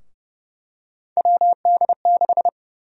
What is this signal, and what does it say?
Morse code, heard as a keyed tone, about 30 wpm, 700 Hz. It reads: WD6
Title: USA Call Sign Prefixes - 30wpm